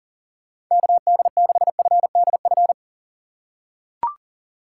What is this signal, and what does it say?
Morse code, heard as a keyed tone, about 40 wpm, 700 Hz. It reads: KD6FDF
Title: USA Call Signs - 6 Characters - 6x Character Spacing - 40wpm